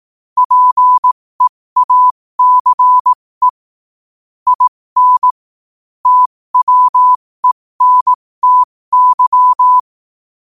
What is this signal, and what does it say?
Morse code, heard as a keyed tone, about 18 wpm, spaced as for 15 wpm, 1000 Hz. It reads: PEACE IN TWENTY
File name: qotdfiles / 15wpm